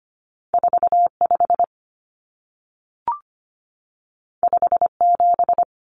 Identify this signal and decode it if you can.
Morse code, heard as a keyed tone, about 25 wpm, 700 Hz. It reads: 45 57
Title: Numbers - 2-Digit 25wpm